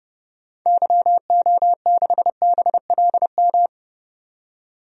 Morse code, heard as a keyed tone, about 30 wpm, 700 Hz. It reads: YO6BLM
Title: DXCC-100 Call Signs - 6 Characters - 30wpm